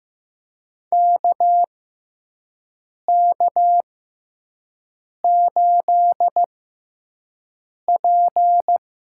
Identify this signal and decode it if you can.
Morse code, heard as a keyed tone, about 15 wpm, spaced as for 5 wpm, 700 Hz. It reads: KK8P
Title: USA Call Signs - 6x Character Spacing - 15wpm